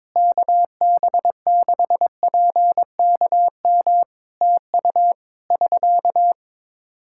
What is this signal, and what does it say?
Morse code, heard as a keyed tone, about 22 wpm, 700 Hz. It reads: KB6PKM TU <SK>
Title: Practice QSO - Exchange 4 - Ending - 22wpm